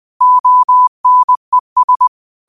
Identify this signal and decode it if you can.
Morse code, heard as a keyed tone, about 20 wpm, 1000 Hz. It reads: ONES